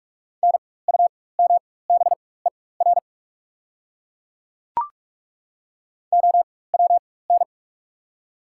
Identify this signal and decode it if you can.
Morse code, heard as a keyed tone, about 43 wpm, spaced as for 21 wpm, 700 Hz. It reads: NUMBER OWN
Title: 4x Character Spacing - Top 500 Words - 45wpm